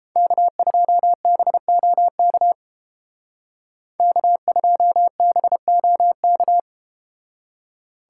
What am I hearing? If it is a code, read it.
Morse code, heard as a keyed tone, about 30 wpm, 700 Hz. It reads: K2BOK K2BOK